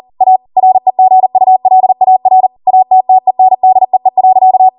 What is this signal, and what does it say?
Morse code, heard as a keyed tone, about 40 wpm, 750 Hz. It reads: A REGULAR ATTENDEE.